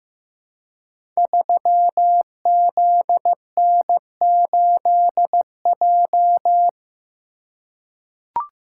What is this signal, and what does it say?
Morse code, heard as a keyed tone, about 15 wpm, 700 Hz. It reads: 3ZN8J
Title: Code Groups - 5 Letter:Numbers - 15wpm